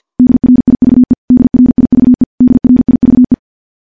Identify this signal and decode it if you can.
Morse code, heard as a keyed tone, about 50 wpm, 250 Hz. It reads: DRIVE DRIVE DRIVE
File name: morse / morse-training-50WPM